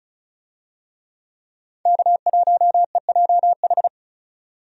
Morse code, heard as a keyed tone, about 34 wpm, 700 Hz. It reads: K1EJH
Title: USA Call Signs - 5 Characters - 35wpm